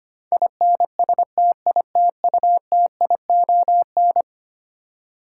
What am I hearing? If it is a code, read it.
Morse code, heard as a keyed tone, about 25 wpm, 700 Hz. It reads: INSTITUTION
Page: morse-code-ninja-practice-25wpm